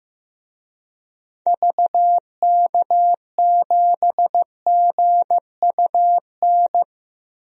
Morse code, heard as a keyed tone, about 16 wpm, 700 Hz. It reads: VK7GUN
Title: DXCC-100 Call Signs - 6 Characters - 15wpm